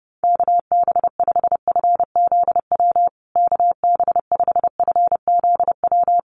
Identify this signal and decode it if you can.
Morse code, heard as a keyed tone, about 30 wpm, 700 Hz. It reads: KB5FZW KB5FZW
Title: USA Call Signs Repeated 3x 30wpm